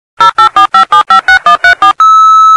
DTMF, heard as telephone touch tones, tones of about 110 ms. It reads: 0845756134